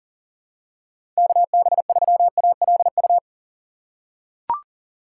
Morse code, heard as a keyed tone, about 40 wpm, 700 Hz. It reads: KB3ALU
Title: USA Call Signs - 6x Character Spacing - 40wpm